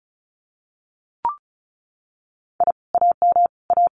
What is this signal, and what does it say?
Morse code, heard as a keyed tone, about 34 wpm, 700 Hz. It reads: I AM A
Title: Common QSO Phrases Encore 35wpm